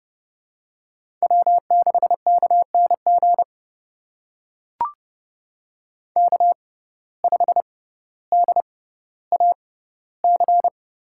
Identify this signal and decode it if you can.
Morse code, heard as a keyed tone, about 30 wpm, spaced as for 15 wpm, 700 Hz. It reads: W6KNG K5DAC
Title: USA Call Signs - 6x Character Spacing - 30wpm